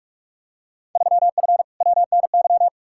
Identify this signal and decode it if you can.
Morse code, heard as a keyed tone, about 46 wpm, 700 Hz. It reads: 3F WNY
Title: Field Day Exchanges - 45wpm